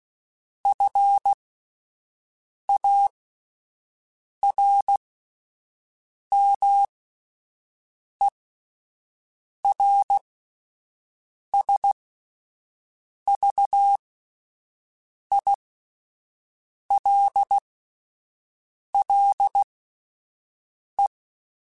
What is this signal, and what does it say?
Morse code, heard as a keyed tone, about 16 wpm, spaced as for 5 wpm, 800 Hz. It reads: FARMERSVILLE
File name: fromSMs / FromSMs mp3s / Cities@5